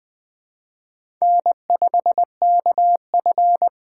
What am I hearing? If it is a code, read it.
Morse code, heard as a keyed tone, about 21 wpm, 700 Hz. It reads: N5KF
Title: USA Call Signs - 4 Characters - 2x Character Spacing - 20wpm